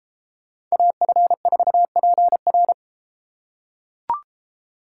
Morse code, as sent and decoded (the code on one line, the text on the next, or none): .- ..-. ....- .--. .-.
AF4PR